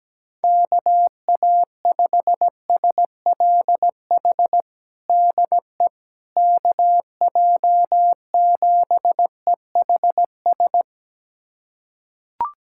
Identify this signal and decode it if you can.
Morse code, heard as a keyed tone, about 17 wpm, 700 Hz. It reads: KA5SLH DE KJ7EHS
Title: USA Callsign DE Callsign 17wpm